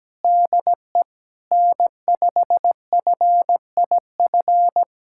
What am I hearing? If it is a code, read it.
Morse code, heard as a keyed tone, about 17 wpm, 700 Hz. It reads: DE N5FIF
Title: USA Callsign DE Callsign 17wpm